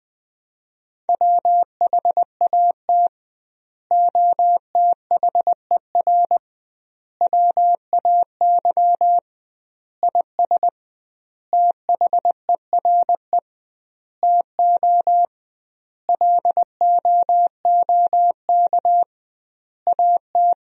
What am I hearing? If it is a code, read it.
Morse code, heard as a keyed tone, about 20 wpm, 700 Hz. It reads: WHAT OTHER WAY IS THERE TO LOOK AT
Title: 2x Word-Spacing - Sentences from Top 100 Words - 20wpm